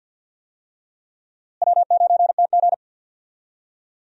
Morse code, heard as a keyed tone, about 50 wpm, 700 Hz. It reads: W9TG
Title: USA Call Signs - 4 Characters - Repeated 2x - 50wpm